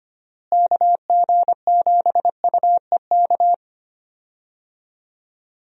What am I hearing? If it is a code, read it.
Morse code, heard as a keyed tone, about 25 wpm, 700 Hz. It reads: KG7UEK
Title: USA Call Signs - 6 Characters - Repeated 3x - 25wpm